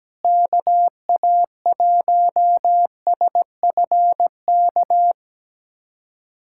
Morse code, heard as a keyed tone, about 17 wpm, 700 Hz. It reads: KA1SFK
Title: USA Callsign DE Callsign 17wpm